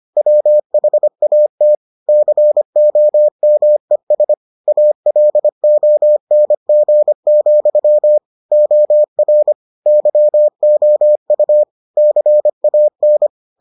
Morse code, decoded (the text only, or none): WHAT COMES ALONG, OR YOU CAN